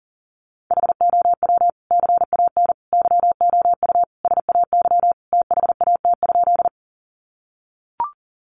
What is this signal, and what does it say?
Morse code, heard as a keyed tone, about 40 wpm, 700 Hz. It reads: HOW CAN YOU SAY THAT?
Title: Sentences from Top 200 Words 40wpm